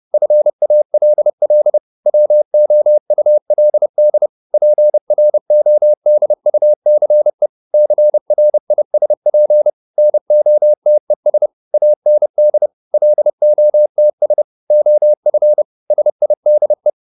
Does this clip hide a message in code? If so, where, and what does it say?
Yes: Morse code from 0.1 to 16.9 s, heard as a keyed tone, about 30 wpm, 600 Hz. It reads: FALL WOULD PRODUCE CRISP NOTES AND LOTS OF SIDE